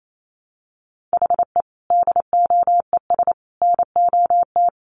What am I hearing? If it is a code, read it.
Morse code, heard as a keyed tone, about 28 wpm, 700 Hz. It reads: HE DOES NOT
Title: Sets of 3 Words 28wpm